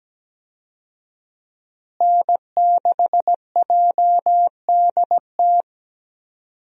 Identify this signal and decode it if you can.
Morse code, heard as a keyed tone, about 17 wpm, 700 Hz. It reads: N6JDT
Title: USA Call Signs - 5 Characters - Repeated 3x - 17wpm